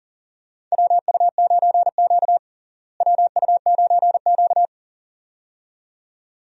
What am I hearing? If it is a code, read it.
Morse code, heard as a keyed tone, about 40 wpm, 700 Hz. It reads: WU9Q WU9Q
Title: USA Call Signs - 4 Characters - Repeated 2x - 40wpm